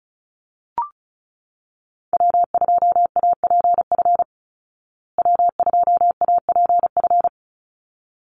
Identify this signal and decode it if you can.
Morse code, heard as a keyed tone, about 34 wpm, 700 Hz. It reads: W2APF W2APF
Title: Call Signs Repeated 3x - 35wpm